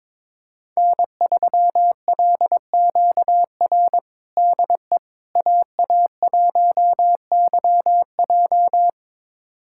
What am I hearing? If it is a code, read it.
Morse code, heard as a keyed tone, about 22 wpm, 700 Hz. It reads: N3LQR DE AA1YJ
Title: USA Callsign DE Callsign - 5 Characters - 22wpm